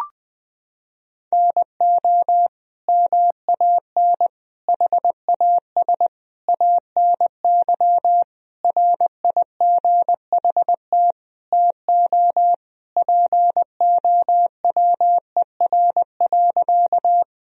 Morse code, heard as a keyed tone, about 21 wpm, 700 Hz. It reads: NO MAN HAS ANY RIGHT TO POWER.
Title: Sentences from Top 1000 Words - Review Part 2 - 20wpm